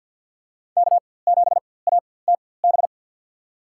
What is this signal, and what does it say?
Morse code, heard as a keyed tone, about 49 wpm, spaced as for 23 wpm, 700 Hz. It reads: K7ATB
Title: USA Call Signs - 5 Characters - 4x Character Spacing - 50wpm